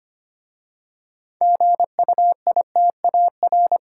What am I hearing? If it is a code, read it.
Morse code, heard as a keyed tone, about 25 wpm, 700 Hz. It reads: GUITAR
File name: practice / 25wpm / Easy Words with Hints - 25wpm